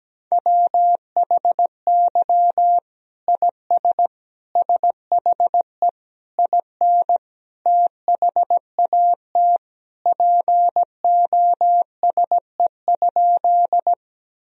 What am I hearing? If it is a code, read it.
Morse code, heard as a keyed tone, about 17 wpm, 700 Hz. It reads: WHY IS SHE IN THAT POSE?
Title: Sentences from Top 1000 Words - Review Part 2 - 17wpm